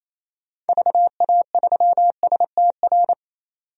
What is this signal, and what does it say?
Morse code, heard as a keyed tone, about 28 wpm, 700 Hz. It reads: VA3STR
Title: DXCC-100 Call Signs - 6 Characters - 28wpm